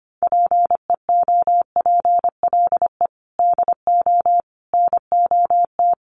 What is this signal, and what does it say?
Morse code, heard as a keyed tone, about 25 wpm, 700 Hz. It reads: PEOPLE DO NOT
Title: Sentences from Top 1200 Words 25wpm